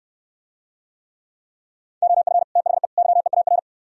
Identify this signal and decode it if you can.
Morse code, heard as a keyed tone, about 68 wpm, 700 Hz. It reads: OF THE YEAR